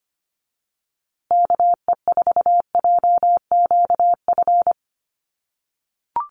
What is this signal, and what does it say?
Morse code, heard as a keyed tone, about 25 wpm, 700 Hz. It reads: KE4JQF
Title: USA Call Signs Repeated 3x 25wpm